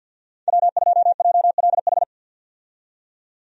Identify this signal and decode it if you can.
Morse code, heard as a keyed tone, about 50 wpm, 700 Hz. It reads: W2JLH
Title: USA Call Signs - 5 Characters - 50wpm